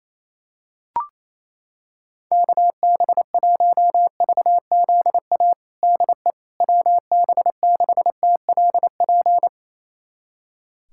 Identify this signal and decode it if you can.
Morse code, heard as a keyed tone, about 28 wpm, 700 Hz. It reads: KB1VZA DE WB6TLP
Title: USA Callsign DE Callsign - 6 Characters - 28wpm